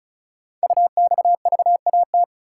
Morse code, heard as a keyed tone, about 36 wpm, 690 Hz. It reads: UXVAT
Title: Code Groups - 5 Letters - 35wpm